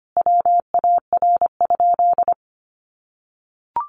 Morse code, heard as a keyed tone, about 25 wpm, 700 Hz. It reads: WAR?
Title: Sentences from Top 1400 Words 25wpm